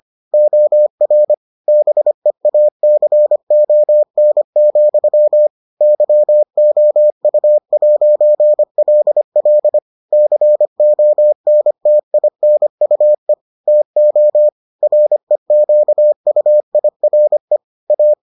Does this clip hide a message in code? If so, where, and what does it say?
Yes: Morse code from 0.3 to 18.1 s, heard as a keyed tone, about 25 wpm, 600 Hz. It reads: OR BEACON, YOU'LL CONTINUE TO REQUIRE A